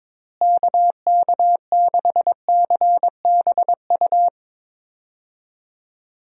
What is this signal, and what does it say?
Morse code, heard as a keyed tone, about 22 wpm, 700 Hz. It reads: KK6CBU
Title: USA Call Signs - 6 Characters - Repeated 3x - 22wpm